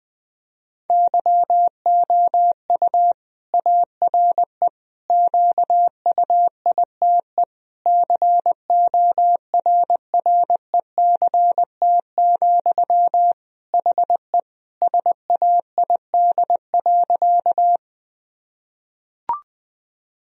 Morse code, heard as a keyed tone, about 20 wpm, 700 Hz. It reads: YOU ARE QUITE CORRECT, HE SAID.
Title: Sentences from Top 1000 Words - Review Part 2 - 20wpm